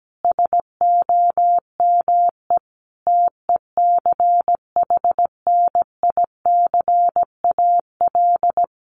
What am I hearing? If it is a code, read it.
Morse code, heard as a keyed tone, about 17 wpm, 700 Hz. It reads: SOME TECHNICAL
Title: Sentences from Top 1100 Words 17wpm